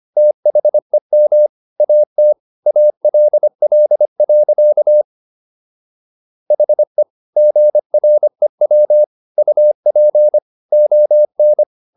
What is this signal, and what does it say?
Morse code, heard as a keyed tone, about 25 wpm, 600 Hz. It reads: THEM AT ALL. HE GREW UP ON